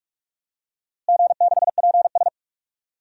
Morse code, heard as a keyed tone, about 44 wpm, 700 Hz. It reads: G6PS
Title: DXCC-100 Call Signs - 4 Characters - Repeated 2x - 45wpm